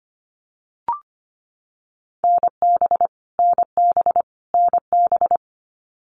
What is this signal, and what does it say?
Morse code, heard as a keyed tone, about 25 wpm, 700 Hz. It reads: NB NB NB
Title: Canadian Provinces and Territories - Repeated 3x - 25wpm